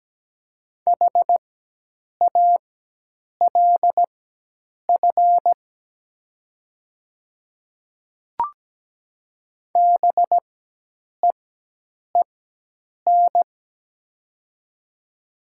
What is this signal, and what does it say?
Morse code, heard as a keyed tone, about 17 wpm, spaced as for 8 wpm, 700 Hz. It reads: HALF BEEN
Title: 4x Character Spacing - Top 500 Words - 17wpm